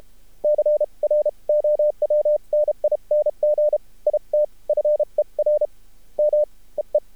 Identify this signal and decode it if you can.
Morse code, heard as a keyed tone, about 32 wpm, spaced as for 27 wpm, 600 Hz. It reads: CROWNING IT FER MEE
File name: archived / CRSnet